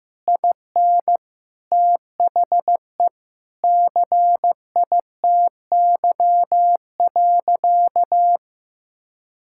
Morse code, heard as a keyed tone, about 16 wpm, 700 Hz. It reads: IN THE CITY.
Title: Sentences from Top 1000 Words - Review Part 6 - 15wpm